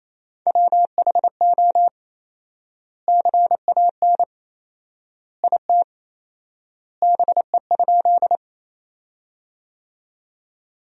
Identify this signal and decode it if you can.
Morse code, heard as a keyed tone, about 28 wpm, 700 Hz. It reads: WHO CAN IT BE?
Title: 4x Word-Spacing - Sentences from Top 100 Words - 28wpm